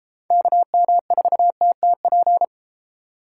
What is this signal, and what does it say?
Morse code, heard as a keyed tone, about 30 wpm, 700 Hz. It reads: KM4TTP